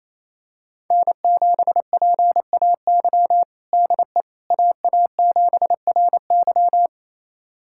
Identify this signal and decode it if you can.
Morse code, heard as a keyed tone, about 28 wpm, 700 Hz. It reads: N7PAY DE AA7RY
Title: USA Callsign DE Callsign - 5 Characters - 28wpm